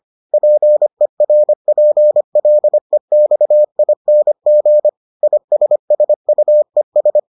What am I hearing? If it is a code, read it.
Morse code, heard as a keyed tone, about 25 wpm, 600 Hz. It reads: PERPLEXING ISSUES